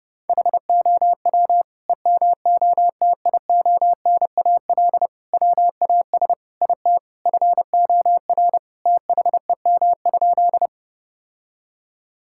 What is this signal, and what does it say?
Morse code, heard as a keyed tone, about 32 wpm, 700 Hz. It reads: HOW EMOTIONAL WAS IT FOR THEM?
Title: Sentences from Top 1000 Words - Review Part 5 - 30wpm